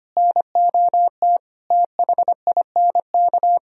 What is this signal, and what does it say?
Morse code, heard as a keyed tone, about 32 wpm, 700 Hz. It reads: NOT THINK